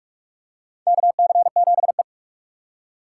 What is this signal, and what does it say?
Morse code, heard as a keyed tone, about 46 wpm, 700 Hz. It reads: KC7E
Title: USA Call Signs - 4 Characters - 3x Character Spacing - 45wpm